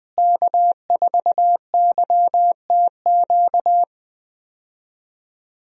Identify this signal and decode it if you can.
Morse code, heard as a keyed tone, about 20 wpm, 700 Hz. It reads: K4YTQ
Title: USA Call Signs - 5 Characters - Repeated 3x - 20wpm